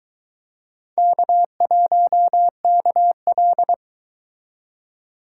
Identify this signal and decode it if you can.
Morse code, heard as a keyed tone, about 24 wpm, 700 Hz. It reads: K1KL